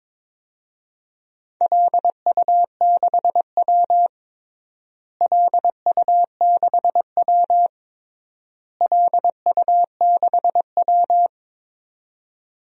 Morse code, heard as a keyed tone, about 22 wpm, 700 Hz. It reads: LU6W LU6W LU6W
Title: DXCC-100 Call Signs - 4 Characters - Repeated 3x - 22wpm